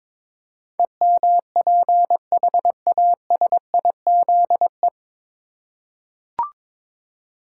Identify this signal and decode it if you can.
Morse code, heard as a keyed tone, about 21 wpm, 700 Hz. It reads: EMPHASIZE
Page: Morse Code Ninja practice 22wpm